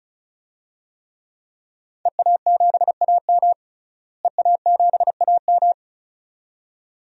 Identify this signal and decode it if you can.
Morse code, heard as a keyed tone, about 36 wpm, 700 Hz. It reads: EA7AM EA7AM
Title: DXCC-100 Call Signs - 5 Characters - Repeated 2x - 35wpm